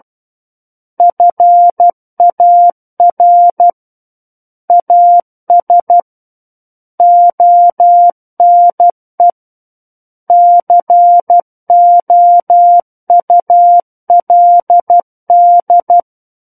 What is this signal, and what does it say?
Morse code, heard as a keyed tone, about 12 wpm, 700 Hz. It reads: FAR AS ONE COULD
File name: Book One / 03-Chapter 3 - 12 WPM